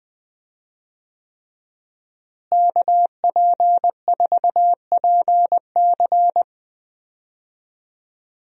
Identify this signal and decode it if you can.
Morse code, heard as a keyed tone, about 20 wpm, 700 Hz. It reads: KP4PC
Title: DXCC-100 Call Signs - 5 Characters - Repeated 3x - 20wpm